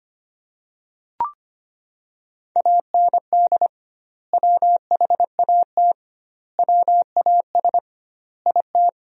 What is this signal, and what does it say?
Morse code, heard as a keyed tone, about 25 wpm, 700 Hz. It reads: AND WHAT WAS IT
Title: 2x Word-Spacing - Sentences from Top 100 Words - 25wpm